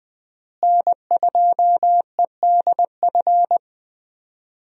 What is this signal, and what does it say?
Morse code, heard as a keyed tone, about 20 wpm, 700 Hz. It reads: N2EDF
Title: USA Call Signs - 5 Characters - Repeated 3x - 20wpm